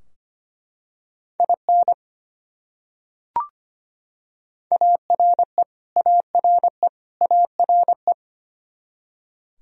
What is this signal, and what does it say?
Morse code, heard as a keyed tone, about 25 wpm, 700 Hz. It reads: IN ARE ARE ARE
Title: Top 100 Words Repeated 25wpm